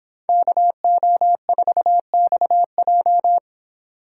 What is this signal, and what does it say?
Morse code, heard as a keyed tone, about 29 wpm, 700 Hz. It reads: KO4XJ